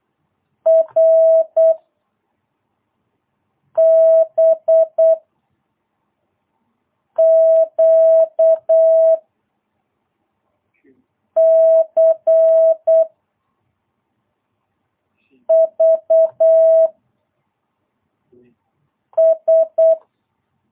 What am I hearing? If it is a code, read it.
Morse code, heard as a keyed tone, about 8 wpm, 650 Hz. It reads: R B Q C V S
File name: jh8cbh / 2022cw